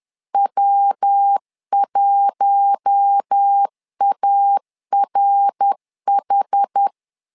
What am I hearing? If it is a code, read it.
Morse code, heard as a keyed tone, about 10 wpm, 800 Hz. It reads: W1ARH
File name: LEZIONI / 4 Q1 10wpm 2448